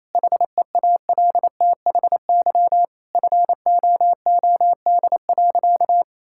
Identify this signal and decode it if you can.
Morse code, heard as a keyed tone, about 28 wpm, 700 Hz. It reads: HEALTHY FOOD.
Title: Sentences from Top 1000 Words - Review Part 5 - 28wpm